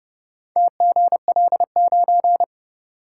Morse code, heard as a keyed tone, about 28 wpm, 700 Hz. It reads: TGL9